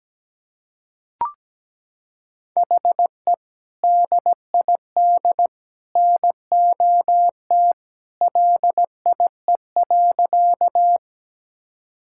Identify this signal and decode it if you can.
Morse code, heard as a keyed tone, about 17 wpm, 700 Hz. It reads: HE DID NOT LIE.
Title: Sentences from Top 1600 Words 17wpm